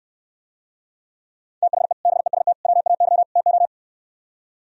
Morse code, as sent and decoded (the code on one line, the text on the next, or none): - .... / -... ... - / .-- .- -.-- / - ---
THE BEST WAY TO